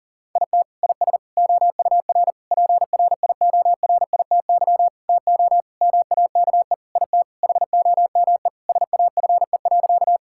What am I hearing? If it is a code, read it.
Morse code, heard as a keyed tone, about 40 wpm, 700 Hz. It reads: IT IS OUR PRIORITY TO MAKE IT HOME SAFE.
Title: Sentences from Top 1600 Words 40wpm